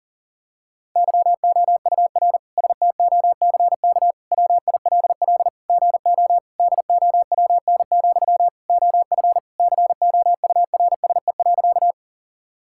Morse code, heard as a keyed tone, about 40 wpm, 700 Hz. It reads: YOUR STOCK WILL GO DOWN, OF COURSE.
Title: Sentences from Top 1000 Words - Review Part 5 - 40wpm